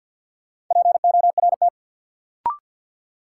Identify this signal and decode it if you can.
Morse code, heard as a keyed tone, about 53 wpm, 700 Hz. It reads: PORT
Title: Sentences from Top 1000 Words - Review Part 1 - New Word Drill - 50wpm